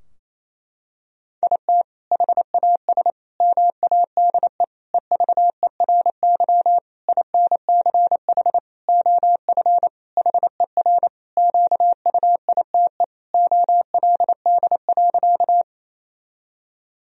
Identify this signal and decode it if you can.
Morse code, heard as a keyed tone, about 28 wpm, 700 Hz. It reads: IT HAS MADE EVERY INCH OF HER QUITE OLD.
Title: Sentences from Top 1000 Words - Review Part 3 - 28wpm